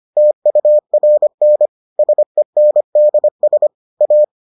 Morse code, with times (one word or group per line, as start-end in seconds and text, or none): 0.2-1.7 TURN
2.0-3.7 SENDS
4.0-4.2 A